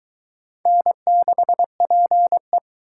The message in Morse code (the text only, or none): N6PE